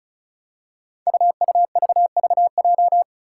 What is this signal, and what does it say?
Morse code, heard as a keyed tone, about 36 wpm, 700 Hz. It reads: UUVVJ
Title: Five Challenging Letter-Number Combinations 35wpm